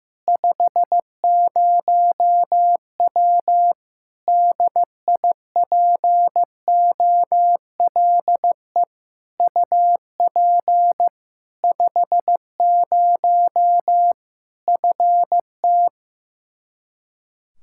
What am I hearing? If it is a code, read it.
Morse code, heard as a keyed tone, about 15 wpm, 700 Hz. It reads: 50W DIPOLE UP 50 FT
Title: Practice QSO - Exchange 2 - Info - 15wpm